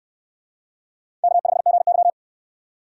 Morse code, heard as a keyed tone, about 68 wpm, 700 Hz. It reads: K5GY